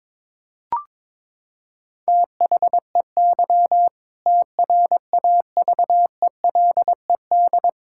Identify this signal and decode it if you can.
Morse code, heard as a keyed tone, about 22 wpm, 700 Hz. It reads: THEY TRAVELED
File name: practice / 22wpm / Sentences with 2-Word Phrases - 22wpm